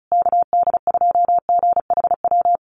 Morse code, heard as a keyed tone, about 34 wpm, 700 Hz. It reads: KD2GHW
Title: USA Call Signs Repeated 3x 35wpm